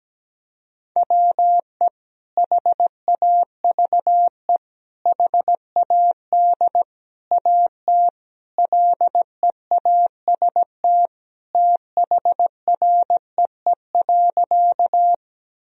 Morse code, heard as a keyed tone, about 17 wpm, 700 Hz. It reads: WE HAVE HAD AT LEAST THREE.
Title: Sentences from Top 1000 Words - Review Part 1 - 17wpm